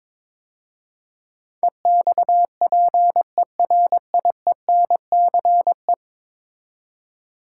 Morse code, heard as a keyed tone, about 22 wpm, 700 Hz. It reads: EXPERIENCE
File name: practice / 22wpm / 10 Letter Words - First 3 Letters Given - 22wpm